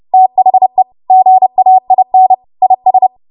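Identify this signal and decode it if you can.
Morse code, heard as a keyed tone, about 30 wpm, 750 Hz. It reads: THE GAIN IS
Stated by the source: W1AW Web Code Practice Files, 30 WPM run from ARRL; text from April 2023 QST.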